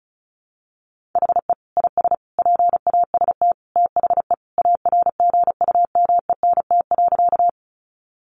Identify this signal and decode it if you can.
Morse code, heard as a keyed tone, about 34 wpm, 700 Hz. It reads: HE IS PAST THE ARGUMENT.
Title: Sentences from Top 1400 Words 35wpm